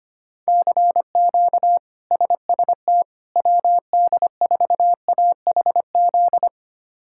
Morse code, heard as a keyed tone, about 25 wpm, 700 Hz. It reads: CQ SST WD4AHZ
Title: K1USN SST QSO Practice - 25wpm